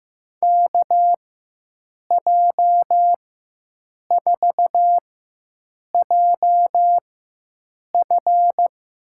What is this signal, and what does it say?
Morse code, heard as a keyed tone, about 15 wpm, spaced as for 7 wpm, 700 Hz. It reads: KJ4JF
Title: USA Call Signs - 6 Characters - 4x Character Spacing - 15wpm